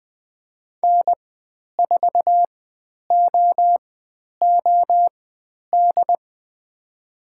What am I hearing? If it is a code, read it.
Morse code, heard as a keyed tone, about 20 wpm, spaced as for 10 wpm, 700 Hz. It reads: N4OOD
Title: 102 Lesson 17g - Letter u - Call Signs 20wpm